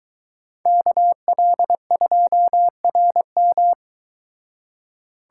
Morse code, heard as a keyed tone, about 22 wpm, 700 Hz. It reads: KL2RM